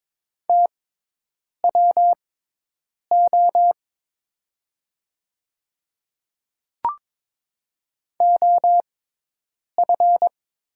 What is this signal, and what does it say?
Morse code, heard as a keyed tone, about 22 wpm, spaced as for 8 wpm, 700 Hz. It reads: TWO OF
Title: 6x Character Spacing - Top 100 Words - 22wpm